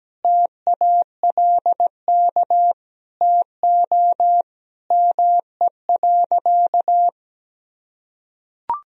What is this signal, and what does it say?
Morse code, heard as a keyed tone, about 17 wpm, 700 Hz. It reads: TALK TO ME.
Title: Sentences from Top 400 Words and Abbreviations 17wpm